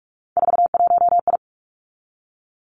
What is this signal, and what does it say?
Morse code, heard as a keyed tone, about 46 wpm, 700 Hz. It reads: 41I
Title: Three Challenging Letter-Number Combinations 45wpm